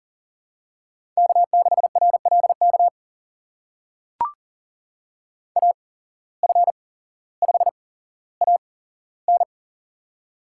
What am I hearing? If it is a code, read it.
Morse code, heard as a keyed tone, about 38 wpm, spaced as for 17 wpm, 700 Hz. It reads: K6RLK AF5AN
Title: USA Call Signs - 5 Characters - 8x Character Spacing - 40wpm